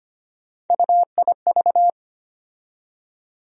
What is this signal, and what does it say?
Morse code, heard as a keyed tone, about 25 wpm, 700 Hz. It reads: UIV
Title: USA Call Sign Postfixes - 3 Letters 25wpm